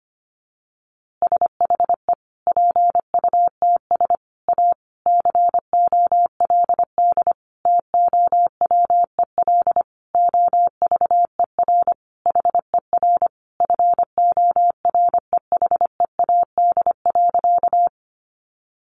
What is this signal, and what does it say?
Morse code, heard as a keyed tone, about 25 wpm, 700 Hz. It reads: SHE PUTS A COLD TOWEL OVER HER FOREHEAD.
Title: Easy Phrases 25wpm